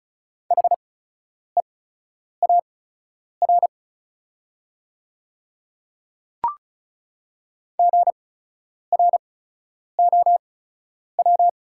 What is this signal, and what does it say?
Morse code, heard as a keyed tone, about 36 wpm, spaced as for 10 wpm, 700 Hz. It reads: HEAR GROW
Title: 8x Character Spacing - Top 300 Words - 35wpm